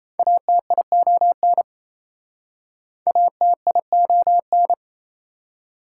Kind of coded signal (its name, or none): Morse code